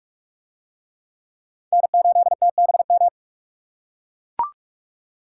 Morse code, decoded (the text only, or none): N8TBM